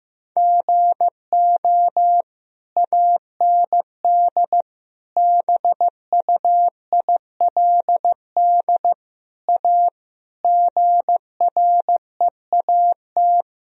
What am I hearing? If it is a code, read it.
Morse code, heard as a keyed tone, about 15 wpm, 700 Hz. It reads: GO AND BUILD A GREAT
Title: Sentences from Top 1000 Words - Review Part 1 - 15wpm